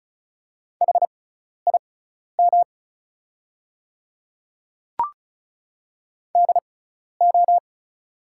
Morse code, heard as a keyed tone, about 34 wpm, spaced as for 12 wpm, 700 Hz. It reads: HIM DO
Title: 6x Character Spacing - Top 200 Words - 35wpm